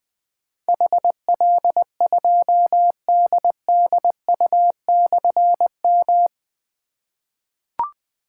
Morse code, heard as a keyed tone, about 21 wpm, 700 Hz. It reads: HL2DDU/M
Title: Difficult Call Signs 20wpm